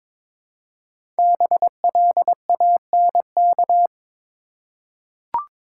Morse code, heard as a keyed tone, about 22 wpm, 700 Hz. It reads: BLANK
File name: 5 Letter Words - First 3 Letters Given - 22wpm